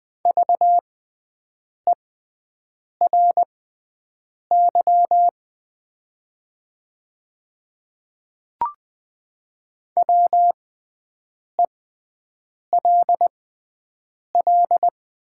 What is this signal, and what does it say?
Morse code, heard as a keyed tone, about 20 wpm, spaced as for 7 wpm, 700 Hz. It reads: VERY WELL
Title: 6x Character Spacing - Top 200 Words - 20wpm